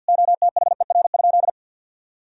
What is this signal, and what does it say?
Morse code, heard as a keyed tone, about 51 wpm, 700 Hz. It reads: OTHER?
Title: Sentences from Top 1000 Words - Review Part 3 - 50wpm